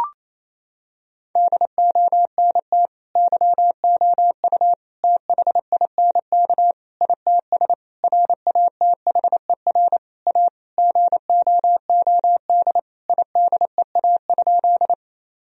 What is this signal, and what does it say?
Morse code, heard as a keyed tone, about 28 wpm, 700 Hz. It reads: DONT YOU THINK ITS RATHER A GOOD IDEA?
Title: Sentences from Top 1000 Words - Review Part 4 - 28wpm